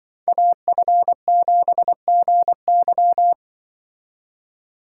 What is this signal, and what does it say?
Morse code, heard as a keyed tone, about 24 wpm, 700 Hz. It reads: AF7GY
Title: Speed Racing - USA 2x2 Call Signs 20wpm